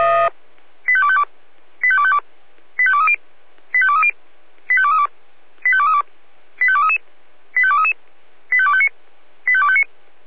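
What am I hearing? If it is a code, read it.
Five-tone selective call (ZVEI), heard as quick runs of five tones, tones of about 70 ms. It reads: 85252 85252 85230 85230 85232 85232 85233 85233 85259 85259
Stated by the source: Received by SDR.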